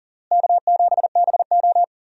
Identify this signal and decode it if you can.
Morse code, heard as a keyed tone, about 37 wpm, 700 Hz. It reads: K7BO
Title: USA Call Signs - 4 Characters - Repeated 3x - 40wpm